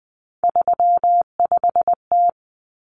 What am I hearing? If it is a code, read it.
Morse code, heard as a keyed tone, about 20 wpm, 700 Hz. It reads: 35T
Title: Common Cut Numbers - 3-Digit 20wpm